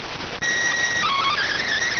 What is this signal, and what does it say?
SSTV (slow-scan television), PD 290.